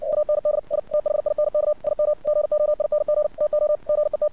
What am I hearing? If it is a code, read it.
Morse code, heard as a keyed tone, about 60 wpm, 600 Hz. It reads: UND I THINK IM GOING TO GET
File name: archived / CRSnet